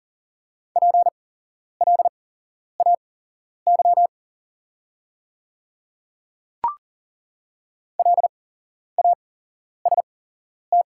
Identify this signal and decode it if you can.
Morse code, heard as a keyed tone, about 40 wpm, spaced as for 11 wpm, 690 Hz. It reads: PLAY LAST
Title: 8x Character Spacing - Top 300 Words - 40wpm